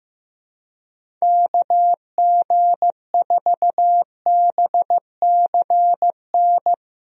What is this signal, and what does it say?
Morse code, heard as a keyed tone, about 15 wpm, 700 Hz. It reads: KG4BCN
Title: USA Call Signs - 6 Characters - Repeated 3x - 15wpm